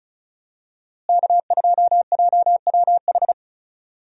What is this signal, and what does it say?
Morse code, heard as a keyed tone, about 34 wpm, 700 Hz. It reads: K2JWH
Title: USA Call Signs - 5 Characters - Repeated 2x - 35wpm